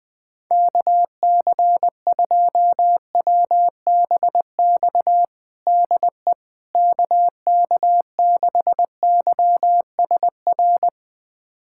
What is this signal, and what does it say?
Morse code, heard as a keyed tone, about 21 wpm, 700 Hz. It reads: KC2WBX DE KK6YSR
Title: USA Callsign DE Callsign - 6 Characters - 20wpm